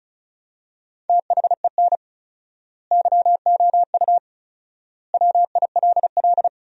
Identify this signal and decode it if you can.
Morse code, heard as a keyed tone, about 36 wpm, 700 Hz. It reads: THEN YOU WILL
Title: 4x Word-Spacing - Sentences from Top 400 Words - 35wpm